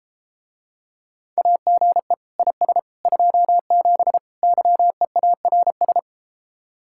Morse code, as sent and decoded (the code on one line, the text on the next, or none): .- --. / .. ... / ..--- --... / -.-- . .- .-. ...
AGE IS 27 YEARS